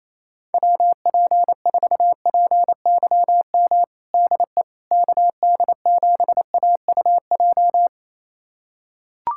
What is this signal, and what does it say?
Morse code, heard as a keyed tone, about 28 wpm, 700 Hz. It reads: WP4PYM DE KD7AUJ
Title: USA Callsign DE Callsign - 6 Characters - 28wpm